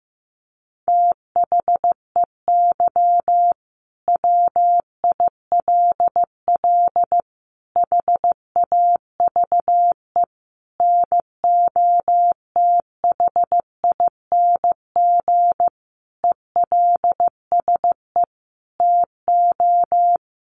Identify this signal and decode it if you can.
Morse code, heard as a keyed tone, about 15 wpm, 700 Hz. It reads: THEY WILL HAVE NOTHING ELSE TO
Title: Sentences from Top 800 Words 15wpm